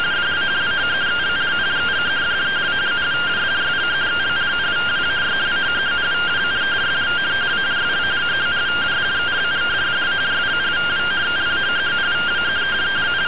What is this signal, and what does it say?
Two-tone teleprinter signal, about 170 Hz shift.